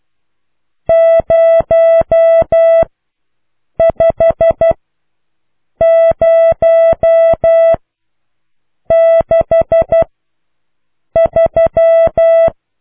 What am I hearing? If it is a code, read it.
Morse code, heard as a keyed tone, about 12 wpm, spaced as for 6 wpm, 660 Hz. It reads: 05063